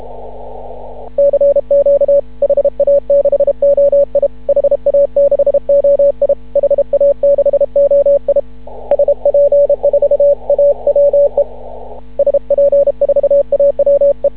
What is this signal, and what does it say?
Morse code, heard as a keyed tone, about 32 wpm, 560 Hz. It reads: CQ HA6OI HA6OI HA6OI SP4AWE SP4AWE
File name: CW / ha6oi 2